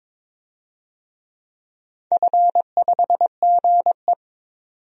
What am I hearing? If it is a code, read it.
Morse code, heard as a keyed tone, about 22 wpm, 700 Hz. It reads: F5GE
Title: DXCC-100 Call Signs - 4 Characters - 22wpm